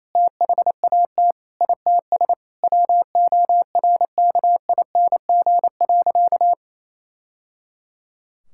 Morse code, heard as a keyed tone, about 28 wpm, 700 Hz. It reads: THAT ITS WORKING.
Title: Sentences from Top 1000 Words - Review Part 6 - 28wpm